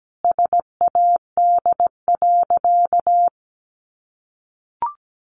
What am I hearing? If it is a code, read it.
Morse code, heard as a keyed tone, about 17 wpm, 700 Hz. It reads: SAD.
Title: Additional Easy Phrases 17wpm